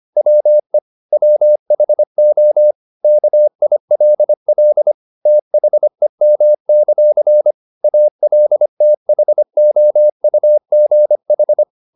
Morse code, heard as a keyed tone, about 25 wpm, 600 Hz. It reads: WE WHO KILL THEM; ALTHOUGH
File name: the old man and the sea part 1 25wpm